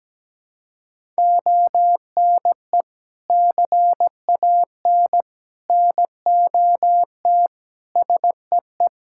Morse code, heard as a keyed tone, about 17 wpm, 700 Hz. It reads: ONE CAN NOT SEE
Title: Sentences from Top 1000 Words - Review Part 5 - 17wpm